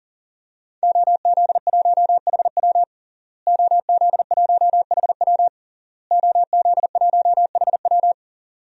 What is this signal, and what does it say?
Morse code, heard as a keyed tone, about 40 wpm, 700 Hz. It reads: OZ1HW OZ1HW OZ1HW
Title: DXCC-100 Call Signs - 5 Characters - Repeated 3x - 40wpm